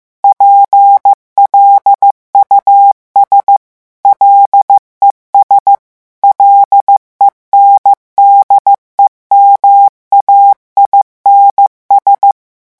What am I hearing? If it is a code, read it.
Morse code, heard as a keyed tone, about 15 wpm, 800 Hz. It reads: PLUS LES LENDEMAINS
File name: morse mp3 / revision9 citations humoristiques vitesse 14 mots minute